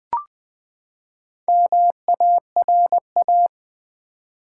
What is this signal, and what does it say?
Morse code, heard as a keyed tone, about 20 wpm, 700 Hz. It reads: MARA
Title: Build A Word - 4 Letters At A Time - 8-Letter Words - 20wpm